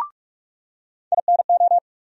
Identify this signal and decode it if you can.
Morse code, heard as a keyed tone, about 47 wpm, 700 Hz. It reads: INO
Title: USA Call Sign Postfixes - 3 Letters 45wpm